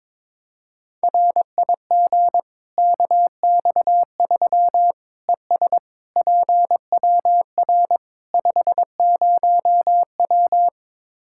Morse code, heard as a keyed tone, about 22 wpm, 700 Hz. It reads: RIG KX3 ES PWR 50W
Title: Practice QSO - Exchange 2 - Info - 22wpm